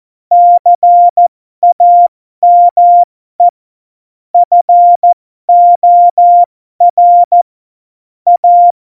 Morse code, heard as a keyed tone, about 14 wpm, spaced as for 12 wpm, 700 Hz. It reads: CAME FOR A